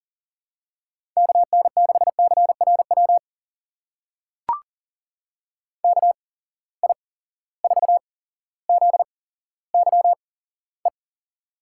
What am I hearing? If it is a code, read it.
Morse code, heard as a keyed tone, about 40 wpm, spaced as for 17 wpm, 700 Hz. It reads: KN6CRW KI4ZYE
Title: USA Call Signs - 6 Characters - 8x Character Spacing - 40wpm